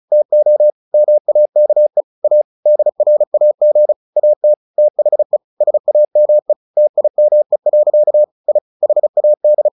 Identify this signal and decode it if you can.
Morse code, heard as a keyed tone, about 34 wpm, 600 Hz. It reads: TO MAKE A DRAG AT THE SAME TIME. I HAD
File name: the old man and the sea 2 35wpm